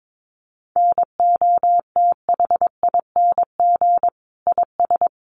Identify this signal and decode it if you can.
Morse code, heard as a keyed tone, about 22 wpm, 700 Hz. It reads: NOTHING IS
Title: Sentences from Top 500 Words 22wpm